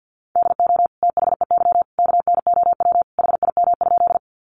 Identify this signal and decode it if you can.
Morse code, heard as a keyed tone, about 50 wpm, 700 Hz. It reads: DO THEY KNOW HIM?
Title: Sentences from Top 100 Words 50wpm